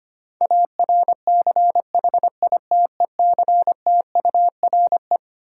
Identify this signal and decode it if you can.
Morse code, heard as a keyed tone, about 25 wpm, 700 Hz. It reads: ARCHITECTURE
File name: Twelve Letter Words - 25wpm